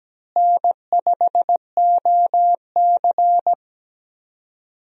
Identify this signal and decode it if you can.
Morse code, heard as a keyed tone, about 17 wpm, 700 Hz. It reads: N5OC
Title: USA Call Signs - 4 Characters - Repeated 2x - 17wpm